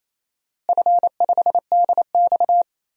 Morse code, heard as a keyed tone, about 27 wpm, 700 Hz. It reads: F5DX
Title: DXCC-100 Call Signs - 4 Characters - Repeated 3x - 28wpm